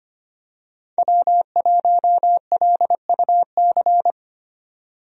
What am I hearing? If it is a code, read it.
Morse code, heard as a keyed tone, about 25 wpm, 700 Hz. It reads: W1LUC
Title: USA Call Signs - 5 Characters - 25wpm